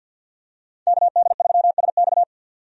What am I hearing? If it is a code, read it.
Morse code, heard as a keyed tone, about 51 wpm, 700 Hz. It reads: KD3SX
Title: USA Call Signs - 5 Characters - 50wpm